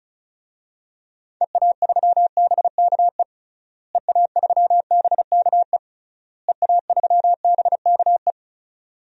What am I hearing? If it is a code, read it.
Morse code, heard as a keyed tone, about 36 wpm, 700 Hz. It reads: EA3BKE EA3BKE EA3BKE
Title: DXCC-100 Call Signs - 6 Characters - Repeated 3x - 35wpm